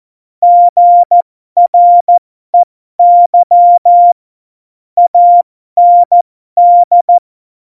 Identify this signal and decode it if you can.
Morse code, heard as a keyed tone, about 14 wpm, spaced as for 12 wpm, 700 Hz. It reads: GREY AND
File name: Morsekurs / Witze